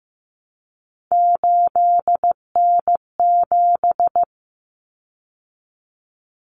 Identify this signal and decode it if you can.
Morse code, heard as a keyed tone, about 15 wpm, 700 Hz. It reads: 8N7
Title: Common Cut Numbers - 3-Digit 15wpm